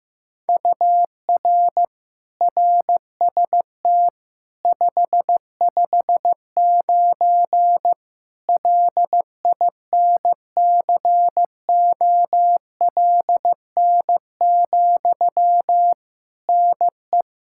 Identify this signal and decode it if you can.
Morse code, heard as a keyed tone, about 15 wpm, 700 Hz. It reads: UR RST 559 LINCOLN, NE
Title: Practice QSO - Exchange 1 - Beginning - 15wpm